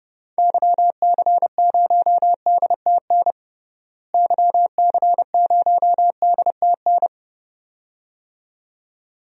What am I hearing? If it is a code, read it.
Morse code, heard as a keyed tone, about 30 wpm, 700 Hz. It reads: YC0DTN YC0DTN
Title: DXCC-100 Call Signs - 6 Characters - Repeated 2x - 30wpm